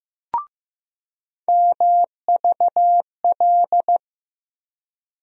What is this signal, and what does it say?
Morse code, heard as a keyed tone, about 15 wpm, 700 Hz. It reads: MVL
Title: USA Call Sign Postfixes - 3 Letters 15wpm